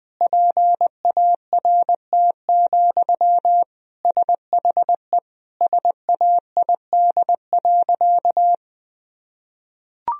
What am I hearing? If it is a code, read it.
Morse code, heard as a keyed tone, about 20 wpm, 700 Hz. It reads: PART, SHE SAID.
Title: Sentences from Top 1000 Words - Review Part 1 - 20wpm